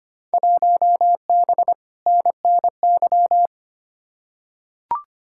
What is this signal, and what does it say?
Morse code, heard as a keyed tone, about 25 wpm, 700 Hz. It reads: 1B NNY
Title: Field Day Exchanges - 25wpm